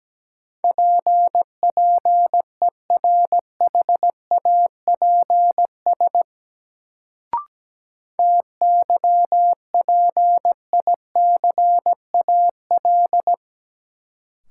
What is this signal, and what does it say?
Morse code, heard as a keyed tone, about 17 wpm, 700 Hz. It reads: PPERHAPS TYPICAL
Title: Sentences from Top 1100 Words - New Word Drill Alt - 17wpm